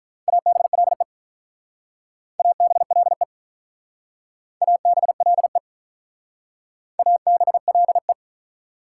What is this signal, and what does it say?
Morse code, heard as a keyed tone, about 44 wpm, 690 Hz. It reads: ABLE ABLE ABLE ABLE
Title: Speed-Racing Prefixes and Suffixes 35wpm